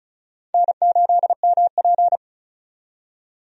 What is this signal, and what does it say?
Morse code, heard as a keyed tone, about 36 wpm, 700 Hz. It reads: N8MP
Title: USA Call Signs - 4 Characters - 35wpm